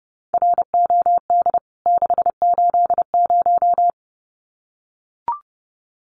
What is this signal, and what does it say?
Morse code, heard as a keyed tone, about 30 wpm, 700 Hz. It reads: ROD 680
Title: CWT Exchange 30wpm